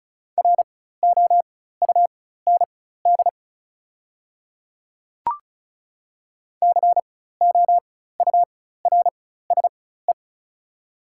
Morse code, heard as a keyed tone, about 34 wpm, spaced as for 16 wpm, 690 Hz. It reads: ROUND COURSE
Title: 4x Character Spacing - Top 500 Words - 35wpm